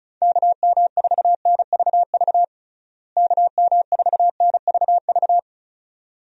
Morse code, heard as a keyed tone, about 34 wpm, 700 Hz. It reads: KM4NVV KM4NVV
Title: USA Call Signs - 6 Characters - Repeated 2x - 35wpm